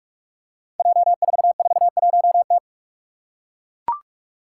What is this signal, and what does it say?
Morse code, heard as a keyed tone, about 43 wpm, 700 Hz. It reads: J441T
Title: Five Challenging Letter-Number Combinations 45wpm